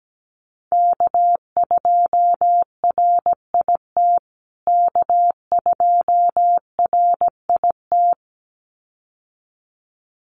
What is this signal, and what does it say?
Morse code, heard as a keyed tone, about 17 wpm, 700 Hz. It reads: K2RIT K2RIT
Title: USA Call Signs Repeated 2x 17wpm